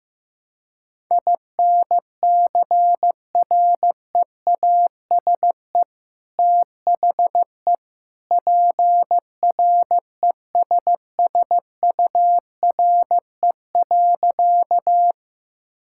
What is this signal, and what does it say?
Morse code, heard as a keyed tone, about 15 wpm, 700 Hz. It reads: INCREASE THE PRESSURE.
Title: Sentences from Top 1000 Words - Review Part 6 - 15wpm